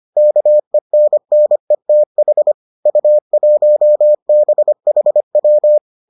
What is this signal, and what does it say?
Morse code, heard as a keyed tone, about 25 wpm, 600 Hz. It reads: KENNETH U1BHW